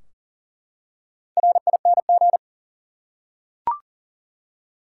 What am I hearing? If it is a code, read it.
Morse code, heard as a keyed tone, about 40 wpm, 700 Hz. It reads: RING
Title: Four Letter Words - 40wpm